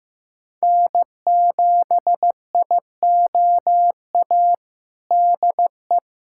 Morse code, heard as a keyed tone, about 15 wpm, 700 Hz. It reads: N7IOA DE
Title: USA Callsign DE Callsign 15wpm